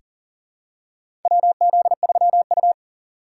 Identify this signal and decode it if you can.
Morse code, heard as a keyed tone, about 37 wpm, 700 Hz. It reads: WZ3U
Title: USA Call Signs - 4 Characters - 4x Character Spacing - 40wpm